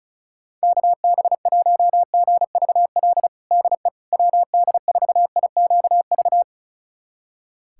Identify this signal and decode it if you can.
Morse code, heard as a keyed tone, about 34 wpm, 700 Hz. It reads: KB1GVL DE WD4IQV
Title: USA Callsign DE Callsign - 6 Characters - 35wpm